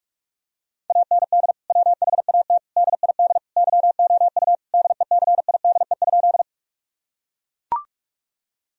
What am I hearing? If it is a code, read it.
Morse code, heard as a keyed tone, about 41 wpm, 700 Hz. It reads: AND WHAT DID YOU DECIDE?
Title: Sentences from Top 1000 Words - Review Part 3 - 45wpm